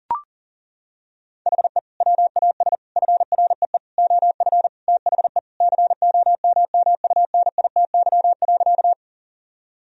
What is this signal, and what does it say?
Morse code, heard as a keyed tone, about 37 wpm, 700 Hz. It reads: HE WAS FREE OF THE COMMUNITY.
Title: Sentences from Top 1000 Words - Review Part 4 - 40wpm